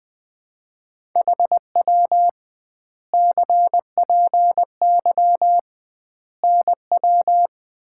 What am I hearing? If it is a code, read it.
Morse code, heard as a keyed tone, about 20 wpm, 700 Hz. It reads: HW CPY NW
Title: Common QSO Phrases - 2x Word-Spacing - 20wpm